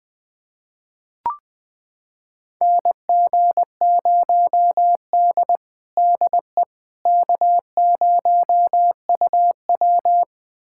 Morse code, heard as a keyed tone, about 20 wpm, 700 Hz. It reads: NG0D DE K0UW
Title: USA Callsign DE Callsign - 4 Characters - 20wpm